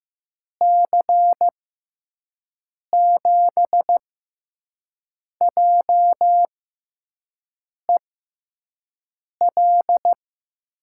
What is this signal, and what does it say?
Morse code, heard as a keyed tone, about 15 wpm, spaced as for 5 wpm, 700 Hz. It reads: C7JEL
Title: USA Call Signs - 6 Characters - 6x Character Spacing - 15wpm